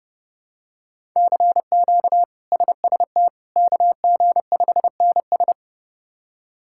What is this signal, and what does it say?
Morse code, heard as a keyed tone, about 32 wpm, 700 Hz. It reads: CQ SST KG5NS
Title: K1USN SST QSO Practice - 30wpm